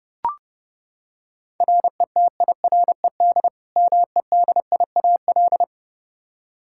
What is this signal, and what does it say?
Morse code, heard as a keyed tone, about 32 wpm, 700 Hz. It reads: RETIRED MEDIAL
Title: Practice QSO - Exchange 3 - More Info - 30wpm